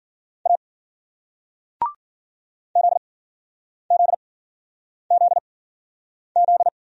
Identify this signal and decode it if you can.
Morse code, heard as a keyed tone, about 50 wpm, spaced as for 9 wpm, 700 Hz. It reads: A ZZZZ